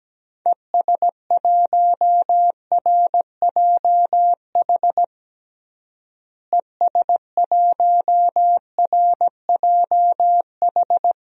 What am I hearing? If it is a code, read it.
Morse code, heard as a keyed tone, about 17 wpm, 700 Hz. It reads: ES1RJH ES1RJH
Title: DXCC-100 Call Signs - 6 Characters - Repeated 2x - 17wpm